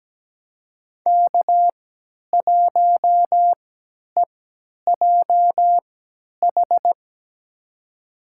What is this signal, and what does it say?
Morse code, heard as a keyed tone, about 17 wpm, spaced as for 10 wpm, 700 Hz. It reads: K1EJH
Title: USA Call Signs - 3x Character Spacing - 17wpm